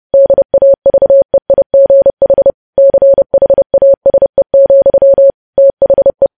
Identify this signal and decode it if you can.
Morse code, heard as a keyed tone, about 30 wpm, 550 Hz. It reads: DAVEIGH CHASE, THE
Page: news-headlines-in-morse-code-at-30-wpm